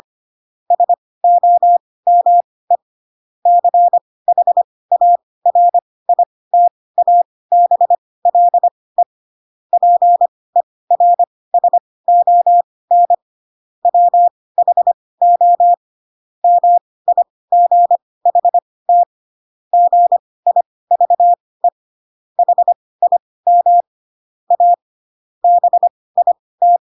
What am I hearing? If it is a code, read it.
Morse code, heard as a keyed tone, about 25 wpm, spaced as for 18 wpm, 700 Hz. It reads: SOME CHARITABLE PERSON WHO MIGHT GIVE HIM A BIT